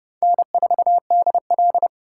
Morse code, heard as a keyed tone, about 31 wpm, 700 Hz. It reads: N4DL